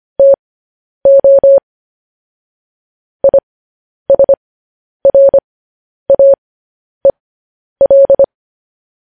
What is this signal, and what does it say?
Morse code, heard as a keyed tone, about 25 wpm, spaced as for 10 wpm, 550 Hz. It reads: TO ISRAEL
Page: news headlines in morse code at 10 wpm